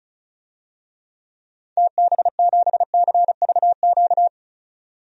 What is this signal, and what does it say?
Morse code, heard as a keyed tone, about 36 wpm, 700 Hz. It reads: TB7CVQ
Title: DXCC-100 Call Signs - 6 Characters - 35wpm